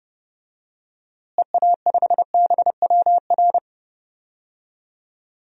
Morse code, heard as a keyed tone, about 32 wpm, 700 Hz. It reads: EA5BWR
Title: DXCC-100 Call Signs - 6 Characters - Repeated 3x - 30wpm